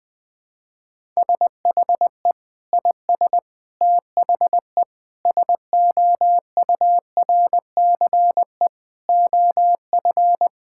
Morse code, heard as a keyed tone, about 21 wpm, 700 Hz. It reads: SHE IS THE SOURCE OF
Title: Sentences from Top 1000 Words - Review Part 5 - 20wpm